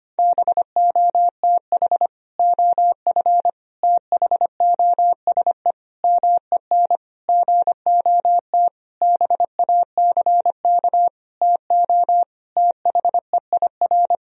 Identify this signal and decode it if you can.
Morse code, heard as a keyed tone, about 25 wpm, 700 Hz. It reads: BOTH OF THOSE MEN GOT BACK TO THEIR
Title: Sentences from Top 1000 Words - Review Part 1 - 25wpm